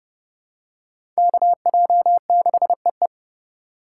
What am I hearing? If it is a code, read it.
Morse code, heard as a keyed tone, about 30 wpm, 700 Hz. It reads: KJ6EE
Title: USA Call Signs - 4x Character Spacing - 30wpm